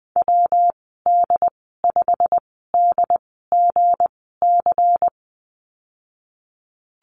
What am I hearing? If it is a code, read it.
Morse code, heard as a keyed tone, about 19 wpm, spaced as for 14 wpm, 700 Hz. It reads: WD5DGC
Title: 207 Lesson 33g - Letter j - Call Signs 20wpm